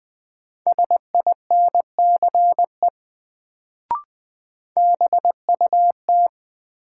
Morse code, heard as a keyed tone, about 20 wpm, 700 Hz. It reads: SINCE BUT
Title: Sets of 2 Words - New Word Drill - 20wpm